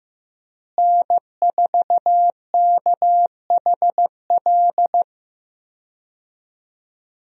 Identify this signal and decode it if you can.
Morse code, heard as a keyed tone, about 15 wpm, 700 Hz. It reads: N4KHL
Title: USA Call Signs - 5 Characters - Repeated 2x - 15wpm